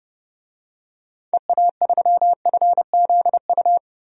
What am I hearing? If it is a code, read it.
Morse code, heard as a keyed tone, about 30 wpm, 700 Hz. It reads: EA3FZU
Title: DXCC-100 Call Signs - 6 Characters - 30wpm